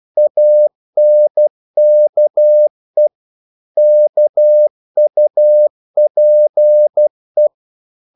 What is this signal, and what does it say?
Morse code, heard as a keyed tone, about 12 wpm, 600 Hz. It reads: ANKE KUPE
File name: Korte ord DK 12wpm